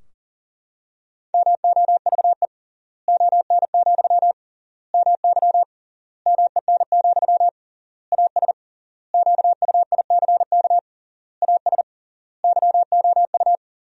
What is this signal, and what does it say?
Morse code, heard as a keyed tone, about 37 wpm, 700 Hz. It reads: MOVE ON, MY MEN, AS QUICK AS YOU
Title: 3x Word-Spacing - Sentences from Top 500 Words - 40wpm